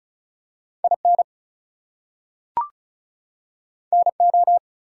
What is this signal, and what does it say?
Morse code, heard as a keyed tone, about 36 wpm, 700 Hz. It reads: IN NO
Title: Two Letter Words - 35wpm